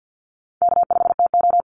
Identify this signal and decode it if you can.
Morse code, heard as a keyed tone, about 48 wpm, 700 Hz. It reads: K5TO
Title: Call Signs Repeated 3x - 50wpm